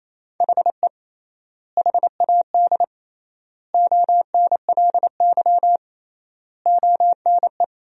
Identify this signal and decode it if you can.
Morse code, heard as a keyed tone, about 28 wpm, 700 Hz. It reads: HE HAD ONLY ONE
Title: 3x Word-Spacing - Sentences from Top 300 Words - 28wpm